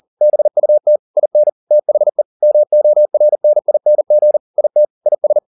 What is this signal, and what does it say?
Morse code, heard as a keyed tone, about 40 wpm, 600 Hz. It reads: BUT IN THE MORNING IT IS